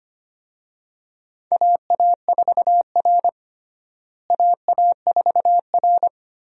Morse code, heard as a keyed tone, about 25 wpm, 700 Hz. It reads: AA4R AA4R
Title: USA Call Signs - 4 Characters - Repeated 2x - 25wpm